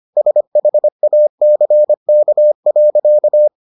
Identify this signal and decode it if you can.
Morse code, heard as a keyed tone, about 25 wpm, 600 Hz. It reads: SHACK.